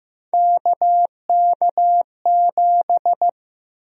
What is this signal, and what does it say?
Morse code, heard as a keyed tone, about 16 wpm, 700 Hz. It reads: KK7
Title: USA Call Sign Prefixes - Repeated 2x - 15wpm